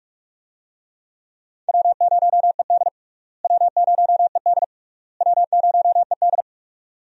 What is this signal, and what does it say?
Morse code, heard as a keyed tone, about 48 wpm, 700 Hz. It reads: W0ED W0ED W0ED
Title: USA Call Signs - 4 Characters - Repeated 3x - 45wpm